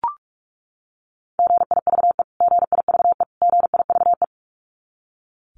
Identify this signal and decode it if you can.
Morse code, heard as a keyed tone, about 44 wpm, 700 Hz. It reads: GIVE GIVE GIVE
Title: Top 300 Words Repeated 45wpm